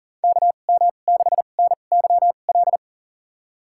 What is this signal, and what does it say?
Morse code, heard as a keyed tone, about 39 wpm, spaced as for 29 wpm, 700 Hz. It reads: KM6NYL
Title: USA Call Signs - 6 Characters - 2x Character Spacing - 40wpm